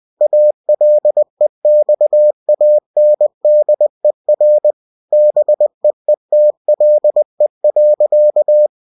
Morse code, heard as a keyed tone, about 20 wpm, 600 Hz. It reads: ALEXANDER BEETLE.
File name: cw-books / winnie-the-pooh / winnie-20wpm-page-0082